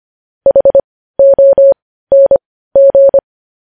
Morse code, heard as a keyed tone, about 25 wpm, spaced as for 15 wpm, 550 Hz. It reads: HONG
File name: podcast / News headlines-1x-25x15